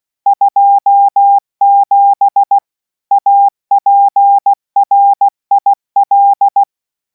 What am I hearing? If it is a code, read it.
Morse code, heard as a keyed tone, about 16 wpm, 800 Hz. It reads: 27 APRIL